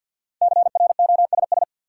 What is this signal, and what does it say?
Morse code, heard as a keyed tone, about 53 wpm, 700 Hz. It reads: CROSS
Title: Sentences from Top 300 Words - New Word Drill - 50wpm